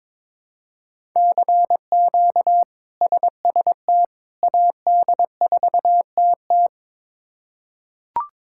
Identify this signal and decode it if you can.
Morse code, heard as a keyed tone, about 22 wpm, 700 Hz. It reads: CQ SST AD4TT
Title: K1USN SST QSO Practice - 22wpm